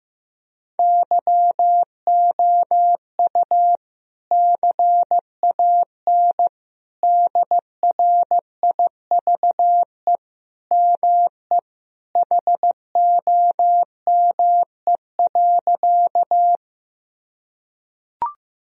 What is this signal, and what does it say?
Morse code, heard as a keyed tone, about 15 wpm, 700 Hz. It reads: YOU CAN DRIVE ME HOME.
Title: Sentences from Top 1000 Words - Review Part 2 - 15wpm